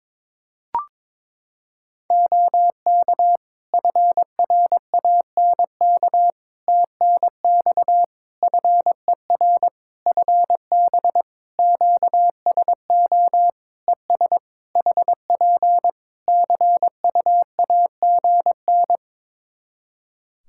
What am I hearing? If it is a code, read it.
Morse code, heard as a keyed tone, about 22 wpm, 700 Hz. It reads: OK FRANK TNX FER FB QSO ES HP CUAGN
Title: Practice QSO - Exchange 4 - Ending - 22wpm